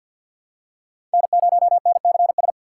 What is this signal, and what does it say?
Morse code, heard as a keyed tone, about 50 wpm, 690 Hz. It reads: N0NCS
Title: USA Call Signs - 5 Characters - 6x Character Spacing - 50wpm